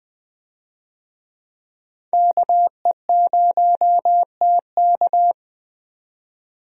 Morse code, heard as a keyed tone, about 20 wpm, 700 Hz. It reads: KE0TK
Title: USA Call Signs - 5 Characters - 20wpm